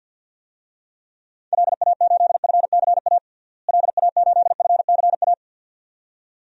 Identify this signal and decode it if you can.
Morse code, heard as a keyed tone, about 53 wpm, 700 Hz. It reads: LA8FCA LA8FCA
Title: DXCC-100 Call Signs - 6 Characters - Repeated 2x - 50wpm